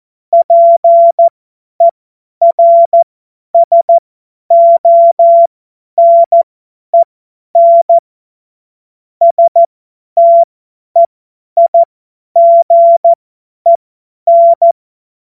Morse code, heard as a keyed tone, about 14 wpm, spaced as for 10 wpm, 680 Hz. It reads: PERSONEN STEIGEN
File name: Morsekurs / Witze / Witz 26. April 2022 50 F70